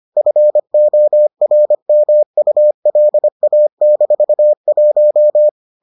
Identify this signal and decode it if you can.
Morse code, heard as a keyed tone, about 25 wpm, 600 Hz. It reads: FORMULA-1